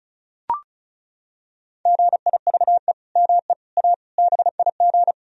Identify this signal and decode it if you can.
Morse code, heard as a keyed tone, about 34 wpm, 700 Hz. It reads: GIVE ME A BIG
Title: Sentences from Top 1000 Words - Review Part 2 - 35wpm